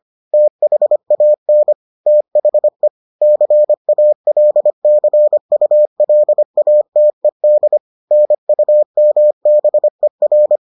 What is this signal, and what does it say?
Morse code, heard as a keyed tone, about 25 wpm, 600 Hz. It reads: THAN THE CALCULATED NUMBER